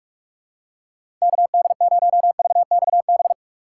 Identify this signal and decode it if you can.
Morse code, heard as a keyed tone, about 41 wpm, 700 Hz. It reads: KD0VXB
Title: USA Call Signs - 6 Characters - 2x Character Spacing - 45wpm